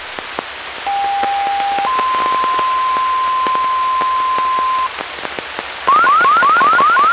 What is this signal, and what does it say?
Two-tone page, heard as a one-second tone, then a three-second tone, about 800 then 1040 Hz, with noise. It is heard from 0.9 to 4.9 s.